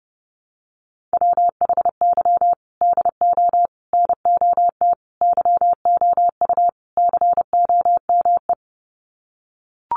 Morse code, heard as a keyed tone, about 30 wpm, 700 Hz. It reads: WHY DO NOT YOU COME
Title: Sets of 5 Words Encore2 30wpm